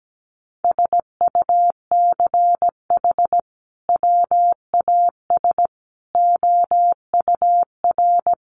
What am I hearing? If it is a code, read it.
Morse code, heard as a keyed tone, about 17 wpm, 700 Hz. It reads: SUCH WAS OUR
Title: Sentences from Top 1200 Words 17wpm